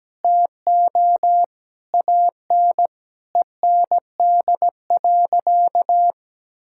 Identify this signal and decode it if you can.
Morse code, heard as a keyed tone, about 17 wpm, 700 Hz. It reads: TO AN END.
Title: Sentences from Top 1000 Words - Review Part 5 - 17wpm